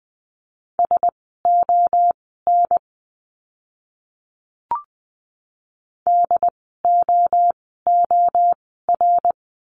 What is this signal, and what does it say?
Morse code, heard as a keyed tone, about 20 wpm, spaced as for 15 wpm, 700 Hz. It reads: SON DOOR
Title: 101 Lesson 17f - Letter u - Words 20wpm